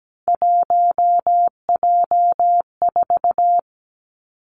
Morse code, heard as a keyed tone, about 17 wpm, 700 Hz. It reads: 1J4
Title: Three Challenging Letter-Number Combinations 17wpm